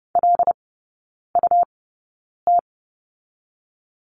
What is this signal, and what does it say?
Morse code, heard as a keyed tone, about 30 wpm, spaced as for 9 wpm, 700 Hz. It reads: LUT
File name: Lesson 18e - Letter m - Three Characters 30wpm